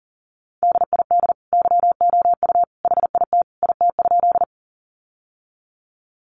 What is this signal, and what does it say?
Morse code, heard as a keyed tone, about 40 wpm, 700 Hz. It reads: DID YOU HIT IT?
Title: Sentences from Top 1200 Words 40wpm